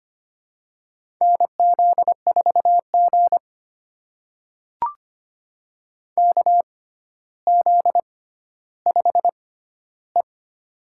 Morse code, heard as a keyed tone, about 25 wpm, spaced as for 13 wpm, 700 Hz. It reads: NZ4G KZ5E
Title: USA Call Signs - 4 Characters - 6x Character Spacing - 25wpm